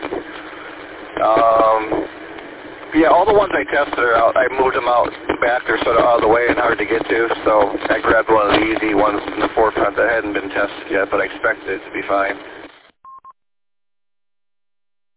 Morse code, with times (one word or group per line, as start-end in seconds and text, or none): none